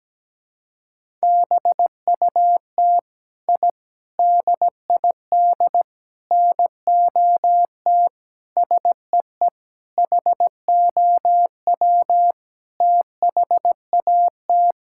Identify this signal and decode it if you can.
Morse code, heard as a keyed tone, about 17 wpm, 700 Hz. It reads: BUT I DID NOT SEE HOW THAT
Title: Sentences from Top 1000 Words - Review Part 5 - 17wpm